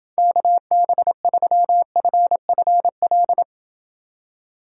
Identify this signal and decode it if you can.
Morse code, heard as a keyed tone, about 29 wpm, 700 Hz. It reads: KB3FFL